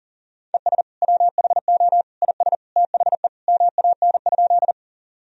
Morse code, heard as a keyed tone, about 40 wpm, 700 Hz. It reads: ES WHO IS THE MAN?
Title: Sentences from Top 200 Words and Abbreviations 40wpm